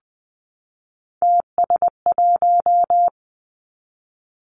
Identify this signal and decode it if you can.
Morse code, heard as a keyed tone, about 20 wpm, 700 Hz. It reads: TS1
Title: Three Challenging Letter-Number Combinations 20wpm